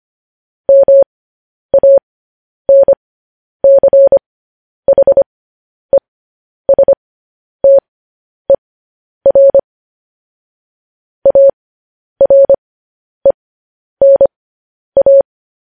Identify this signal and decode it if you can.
Morse code, heard as a keyed tone, about 25 wpm, spaced as for 10 wpm, 550 Hz. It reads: MANCHESTER ARENA
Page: news headlines in morse code at 10 wpm